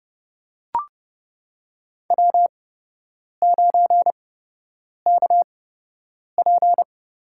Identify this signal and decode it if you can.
Morse code, heard as a keyed tone, about 30 wpm, spaced as for 8 wpm, 700 Hz. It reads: W9KP
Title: USA Call Signs - 4 Characters - 8x Character Spacing - 30wpm